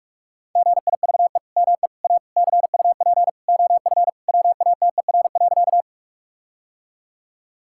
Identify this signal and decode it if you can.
Morse code, heard as a keyed tone, about 46 wpm, 700 Hz. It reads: GIVE ME A CUP OF WATER.
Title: Sentences from Top 1600 Words 45wpm